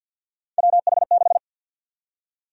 Morse code, heard as a keyed tone, about 52 wpm, 700 Hz. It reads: WH6
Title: USA Call Sign Prefixes - Repeated 2x - 50wpm